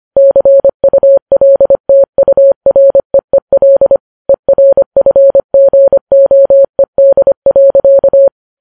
Morse code, heard as a keyed tone, about 25 wpm, 550 Hz. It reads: CULTUREEL ERFGOED.
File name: Files Web tls / Morse on9fv / morse25kps